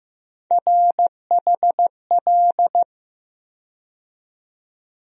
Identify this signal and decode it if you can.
Morse code, heard as a keyed tone, about 15 wpm, 700 Hz. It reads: RHL
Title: USA Call Sign Postfixes - 3 Letters 15wpm